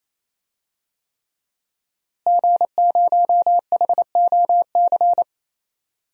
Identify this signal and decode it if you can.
Morse code, heard as a keyed tone, about 28 wpm, 700 Hz. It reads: G0HOC
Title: DXCC-100 Call Signs - 5 Characters - Repeated 3x - 28wpm